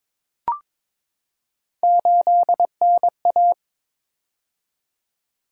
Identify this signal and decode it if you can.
Morse code, heard as a keyed tone, about 22 wpm, 700 Hz. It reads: 8NA
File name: practice / 22wpm / Three Letter-Number Combinations - 22wpm